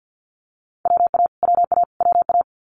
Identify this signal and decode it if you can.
Morse code, heard as a keyed tone, about 50 wpm, 690 Hz. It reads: WA WA WA
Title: Multi-letter Phonemes Repeated 3x 50wpm